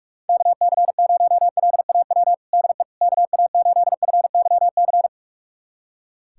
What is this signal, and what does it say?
Morse code, heard as a keyed tone, about 46 wpm, 700 Hz. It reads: KC0LAW DE KA8FYC
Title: USA Callsign DE Callsign - 6 Characters - 45wpm